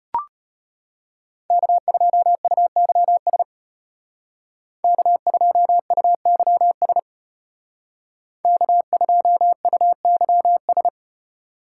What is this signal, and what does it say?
Morse code, heard as a keyed tone, about 33 wpm, 700 Hz. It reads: K2UYS K2UYS K2UYS